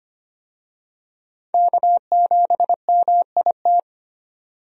Morse code, heard as a keyed tone, about 25 wpm, 700 Hz. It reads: K7MIT
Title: USA Call Signs - 5 Characters - 25wpm